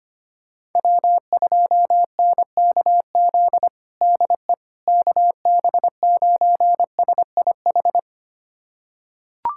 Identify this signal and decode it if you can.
Morse code, heard as a keyed tone, about 25 wpm, 700 Hz. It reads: W2NKZ DE KB9SIH
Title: USA Callsign DE Callsign 25wpm